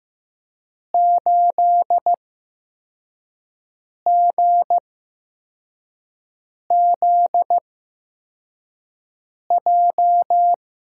Morse code, heard as a keyed tone, about 15 wpm, 700 Hz. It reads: 8 G Z J
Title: USA Call Signs - 5 Characters - 8x Character Spacing - 15wpm